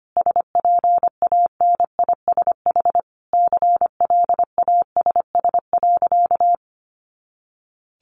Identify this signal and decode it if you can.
Morse code, heard as a keyed tone, about 25 wpm, 700 Hz. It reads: SPANISH CLASS.
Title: Additional Easy Phrases 25wpm